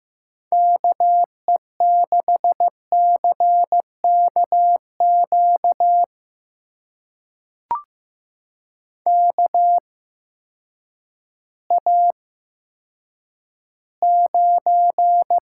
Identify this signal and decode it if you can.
Morse code, heard as a keyed tone, about 15 wpm, spaced as for 9 wpm, 700 Hz. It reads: KE6CKQ KA9
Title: USA Call Signs - 6 Characters - 8x Character Spacing - 15wpm